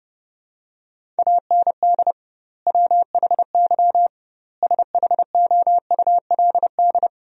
Morse code, heard as a keyed tone, about 30 wpm, 700 Hz. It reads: AND WHY SHOULD
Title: 2x Word-Spacing - Sentences from Top 300 Words - 30wpm